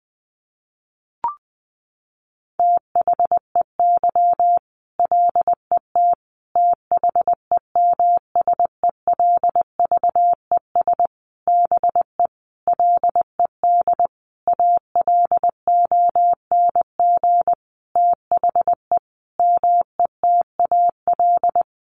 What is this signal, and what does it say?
Morse code, heard as a keyed tone, about 20 wpm, 700 Hz. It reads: THEY LET THEMSELVES BE LED ALONG THE METAL
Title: A Sound Of Thunder 20wpm